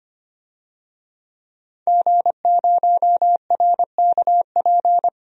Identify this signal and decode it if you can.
Morse code, heard as a keyed tone, about 25 wpm, 700 Hz. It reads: G0RKP
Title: DXCC-100 Call Signs - 5 Characters - 25wpm